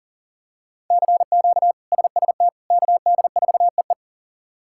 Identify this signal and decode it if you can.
Morse code, heard as a keyed tone, about 40 wpm, 700 Hz. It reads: CQ SST KD4EE
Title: K1USN SST QSO Practice - 40wpm